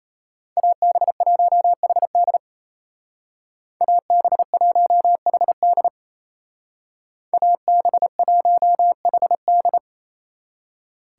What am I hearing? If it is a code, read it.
Morse code, heard as a keyed tone, about 32 wpm, 700 Hz. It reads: AB1HD AB1HD AB1HD
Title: Speed Racing - USA 2x2 Call Signs 28wpm